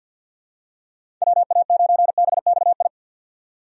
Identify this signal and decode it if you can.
Morse code, heard as a keyed tone, about 46 wpm, 700 Hz. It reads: WA9BXI
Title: USA Call Signs - 6 Characters - 50wpm